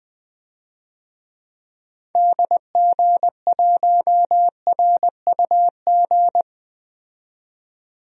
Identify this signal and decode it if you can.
Morse code, heard as a keyed tone, about 20 wpm, 700 Hz. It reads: DG1RUG
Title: DXCC-100 Call Signs - 6 Characters - 20wpm